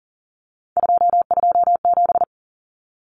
Morse code, heard as a keyed tone, about 40 wpm, 700 Hz. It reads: 227
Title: Common Cut Numbers - 3-Digit 40wpm